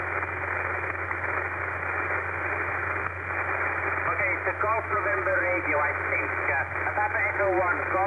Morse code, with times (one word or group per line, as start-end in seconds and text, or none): none